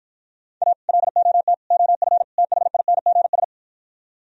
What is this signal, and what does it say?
Morse code, heard as a keyed tone, about 52 wpm, 700 Hz. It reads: A LOT OF THINGS